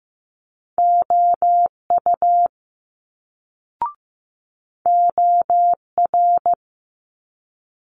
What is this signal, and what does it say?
Morse code, heard as a keyed tone, about 15 wpm, 700 Hz. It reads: OU OR
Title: Multi-letter Phonemes Quick Fire 15wpm